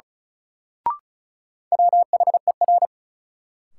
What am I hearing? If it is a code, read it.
Morse code, heard as a keyed tone, about 36 wpm, 700 Hz. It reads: WHER
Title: Sentences from Top 1500 Words - New Word Drill Alt - 35wpm